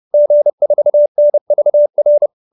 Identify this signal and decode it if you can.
Morse code, heard as a keyed tone, about 30 wpm, 590 Hz. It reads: G4NVR
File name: QSO315 30